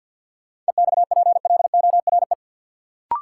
Morse code, heard as a keyed tone, about 52 wpm, 700 Hz. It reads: EXPLORE
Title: Sentences from Top 1100 Words - New Word Drill - 50wpm